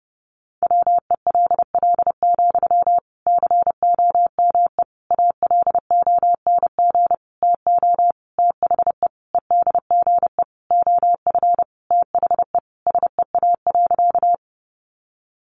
Morse code, heard as a keyed tone, about 30 wpm, 700 Hz. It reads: WELL, COME ALONG TO THE EDGE OF THE SEA.
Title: Sentences from Top 1300 Words 30wpm